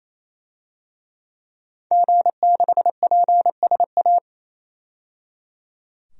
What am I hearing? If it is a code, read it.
Morse code, heard as a keyed tone, about 28 wpm, 700 Hz. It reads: G6PSA
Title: DXCC-100 Call Signs - 5 Characters - 28wpm